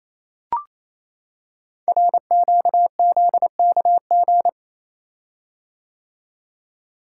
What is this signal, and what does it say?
Morse code, heard as a keyed tone, about 27 wpm, 700 Hz. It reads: RQZKG
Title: Code Groups - 5 Letter:Numbers - 28wpm